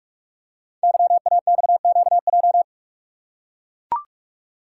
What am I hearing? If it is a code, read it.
Morse code, heard as a keyed tone, about 44 wpm, 690 Hz. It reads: YAXQJ
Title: Code Groups - 5 Letters - 45wpm